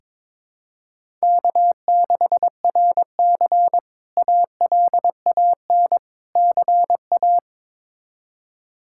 Morse code, heard as a keyed tone, about 22 wpm, 700 Hz. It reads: K6RC ALAN CA
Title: K1USN SST QSO Practice - 22wpm